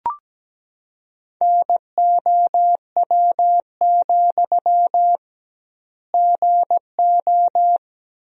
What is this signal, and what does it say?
Morse code, heard as a keyed tone, about 17 wpm, 700 Hz. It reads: NOW, GO
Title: 2x Word-Spacing - Sentences from Top 200 Words - 17wpm